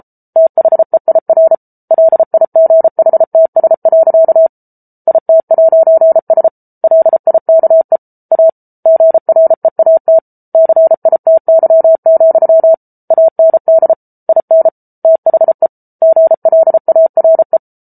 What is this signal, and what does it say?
Morse code, heard as a keyed tone, about 34 wpm, 650 Hz. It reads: THEIR LIGHTS. IT'S LIKE A GREAT CITY, AND IN THE GLARE